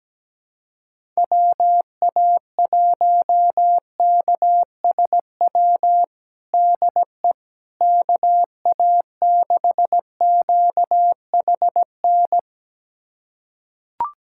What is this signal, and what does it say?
Morse code, heard as a keyed tone, about 17 wpm, 700 Hz. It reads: WA1KSW DE KA6QHN
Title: USA Callsign DE Callsign - 6 Characters - 17wpm